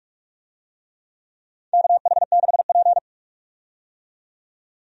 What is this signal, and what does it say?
Morse code, heard as a keyed tone, about 46 wpm, 700 Hz. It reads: KH6P
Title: DXCC-100 Call Signs - 4 Characters - 45wpm